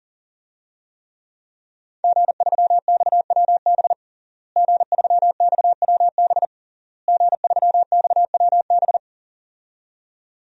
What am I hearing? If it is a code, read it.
Morse code, heard as a keyed tone, about 37 wpm, 700 Hz. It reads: G3XWB G3XWB G3XWB
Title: DXCC-100 Call Signs - 5 Characters - Repeated 3x - 40wpm